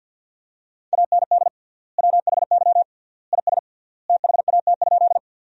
Morse code, heard as a keyed tone, about 48 wpm, 700 Hz. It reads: AND WHY IS THAT?
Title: 3x Word-Spacing - Sentences from Top 200 Words - 50wpm